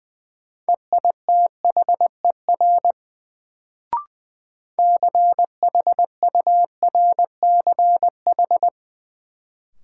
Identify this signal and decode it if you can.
Morse code, heard as a keyed tone, about 20 wpm, 700 Hz. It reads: EITHER CHURCH
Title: Sets of 5 Words - New Word Drill - 20wpm